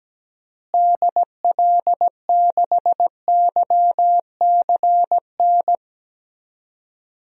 Morse code, heard as a keyed tone, about 17 wpm, 700 Hz. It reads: DL6YCN
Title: DXCC-100 Call Signs - 6 Characters - Repeated 3x - 17wpm